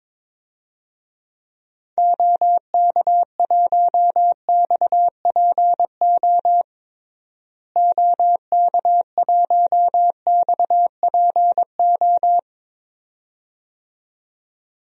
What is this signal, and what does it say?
Morse code, heard as a keyed tone, about 22 wpm, 700 Hz. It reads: OK1XPO OK1XPO
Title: DXCC-100 Call Signs - 6 Characters - Repeated 2x - 22wpm